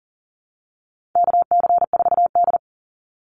Morse code, heard as a keyed tone, about 40 wpm, 700 Hz. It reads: KC4D
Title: Call Signs Repeated 3x - 40wpm